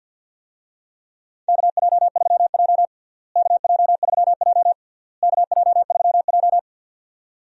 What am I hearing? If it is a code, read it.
Morse code, heard as a keyed tone, about 52 wpm, 700 Hz. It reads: KJ3J KJ3J KJ3J
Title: USA Call Signs - 4 Characters - Repeated 3x - 50wpm